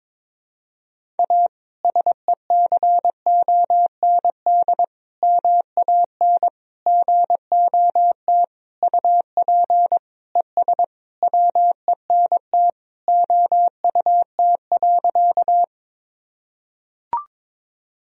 Morse code, heard as a keyed tone, about 22 wpm, 700 Hz. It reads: A SECOND MAN GOT UP ES WENT OUT.
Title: Sentences from Top 300 Words and Abbreviations 22wpm